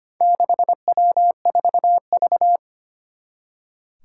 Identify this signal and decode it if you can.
Morse code, heard as a keyed tone, about 25 wpm, 700 Hz. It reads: 6W4V
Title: Four Challenging Letter-Number Combinations 25wpm